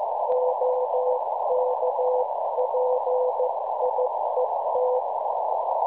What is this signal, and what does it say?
Morse code, heard as a keyed tone, about 15 wpm, 500 Hz, with noise. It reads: OK PIET
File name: cwronde / websdr recording .6kHz